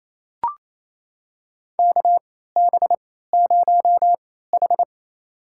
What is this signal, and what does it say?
Morse code, heard as a keyed tone, about 28 wpm, spaced as for 16 wpm, 700 Hz. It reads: KB0H
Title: USA Call Signs - 4 Characters - 3x Character Spacing - 28wpm